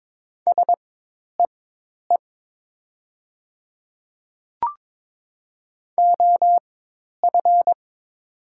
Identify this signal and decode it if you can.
Morse code, heard as a keyed tone, about 22 wpm, spaced as for 10 wpm, 700 Hz. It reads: SEE OF
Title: 4x Character Spacing - Top 100 Words - 22wpm